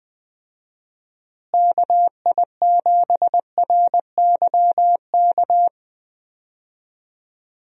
Morse code, heard as a keyed tone, about 20 wpm, 700 Hz. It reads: KI7RYK
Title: USA Call Signs - 6 Characters - Repeated 2x - 20wpm